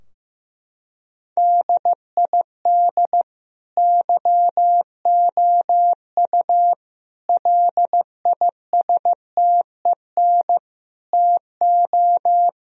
Morse code, heard as a keyed tone, about 16 wpm, 700 Hz. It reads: DID YOU LISTEN TO
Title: Sentences from Top 1000 Words - Review Part 4 - 15wpm